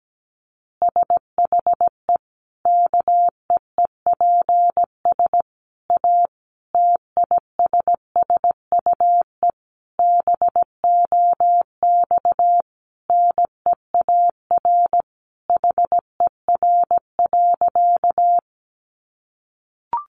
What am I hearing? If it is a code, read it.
Morse code, heard as a keyed tone, about 17 wpm, 700 Hz. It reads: SHE KEEPS A TISSUE BOX NEAR HER.
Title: Easy Phrases 17wpm